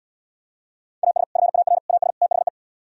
Morse code, heard as a keyed tone, about 77 wpm, 700 Hz. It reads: AS FAR AS THE